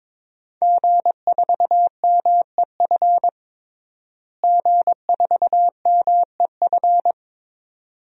Morse code, heard as a keyed tone, about 22 wpm, 700 Hz. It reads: G4MEF G4MEF
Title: DXCC-100 Call Signs - 5 Characters - Repeated 2x - 22wpm